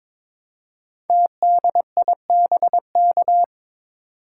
Morse code, heard as a keyed tone, about 22 wpm, 700 Hz. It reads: TDIBK
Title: Code Groups - 5 Letters - 22wpm